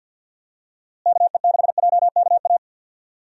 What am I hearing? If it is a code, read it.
Morse code, heard as a keyed tone, about 47 wpm, 700 Hz. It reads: KE6JKA
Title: USA Call Signs - 6 Characters - Repeated 2x - 50wpm